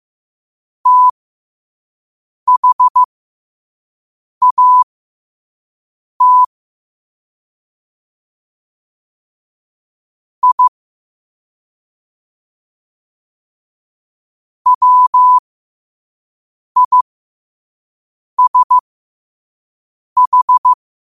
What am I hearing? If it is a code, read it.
Morse code, heard as a keyed tone, about 15 wpm, spaced as for 5 wpm, 1000 Hz. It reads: THAT I WISH
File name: qotdfiles / 5wpm